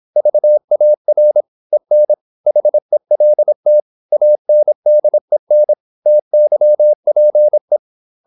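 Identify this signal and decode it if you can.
Morse code, heard as a keyed tone, about 26 wpm, 600 Hz. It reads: VAR EN HELT ANDEN TYPE
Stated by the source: High-speed (QRQ) Morse at 26 wpm.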